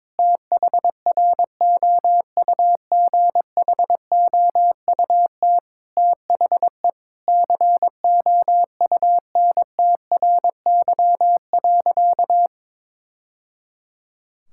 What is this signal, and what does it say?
Morse code, heard as a keyed tone, about 22 wpm, 700 Hz. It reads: THROUGHOUT THE COUNTRY.
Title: Sentences from Top 1000 Words - Review Part 5 - 22wpm